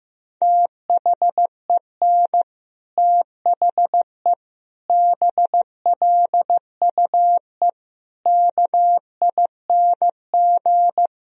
Morse code, heard as a keyed tone, about 16 wpm, 700 Hz. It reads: THEN THE BLUE KING
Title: Sentences from Top 1000 Words - Review Part 3 - 15wpm